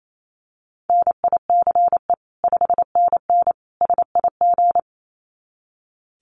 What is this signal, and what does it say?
Morse code, heard as a keyed tone, about 28 wpm, 700 Hz. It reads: NICE 5NN SIG
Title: Common QSO Phrases 28wpm